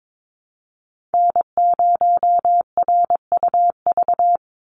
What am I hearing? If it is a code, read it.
Morse code, heard as a keyed tone, about 22 wpm, 700 Hz. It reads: N0RUV
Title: USA Call Signs 22wpm